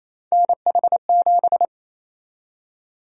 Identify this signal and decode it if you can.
Morse code, heard as a keyed tone, about 28 wpm, 700 Hz. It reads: NH7
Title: USA Call Sign Prefixes - 28wpm